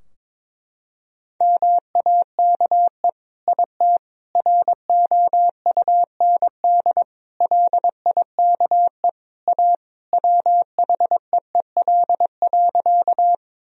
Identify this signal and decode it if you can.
Morse code, heard as a keyed tone, about 22 wpm, 700 Hz. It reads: MAKE IT ROUND LIKE A WHEEL.
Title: Sentences from Top 1000 Words - Review Part 3 - 22wpm